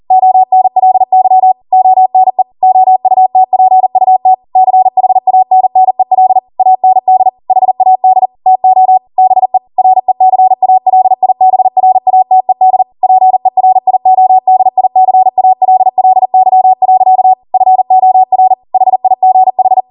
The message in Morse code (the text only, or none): ONLY ONE OUTPUT CHANNEL AND HAD TO BE RECALIBRATED PERIODICALLY. FOR HIGH